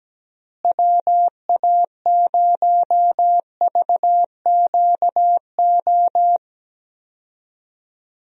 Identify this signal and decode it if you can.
Morse code, heard as a keyed tone, about 17 wpm, 700 Hz. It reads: WA0VQO
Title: USA Call Signs - 6 Characters - Repeated 3x - 17wpm